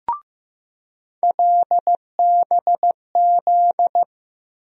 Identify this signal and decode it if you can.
Morse code, heard as a keyed tone, about 15 wpm, 700 Hz. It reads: LBZ
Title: USA Call Sign Postfixes - 3 Letters 15wpm